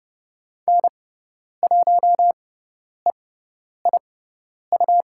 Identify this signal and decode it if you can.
Morse code, heard as a keyed tone, about 30 wpm, spaced as for 10 wpm, 700 Hz. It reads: N1EIU
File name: Lesson 18g - Letter m - Call Signs 30wpm